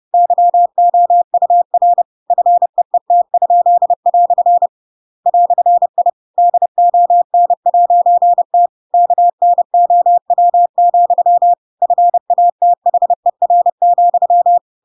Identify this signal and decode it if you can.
Morse code, heard as a keyed tone, about 30 wpm, 700 Hz. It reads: YOUR FEET?" "I DON'T KNOW, FATHER,
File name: Pinocchio / Pinocchio ch7 30 30 ik1hge